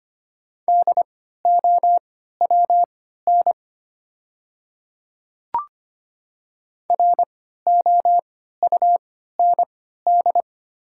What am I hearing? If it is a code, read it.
Morse code, heard as a keyed tone, about 25 wpm, spaced as for 14 wpm, 700 Hz. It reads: DOWN ROUND
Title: 3x Character Spacing - Top 200 Words - 25wpm